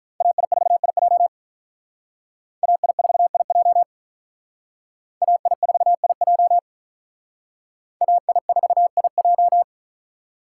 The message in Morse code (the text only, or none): AI4IJ AI4IJ AI4IJ AI4IJ